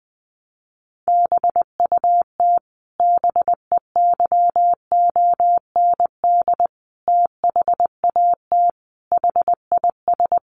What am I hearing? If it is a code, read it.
Morse code, heard as a keyed tone, about 20 wpm, 700 Hz. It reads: BUT BEYOND THAT HIS
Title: Sentences from Top 1300 Words 20wpm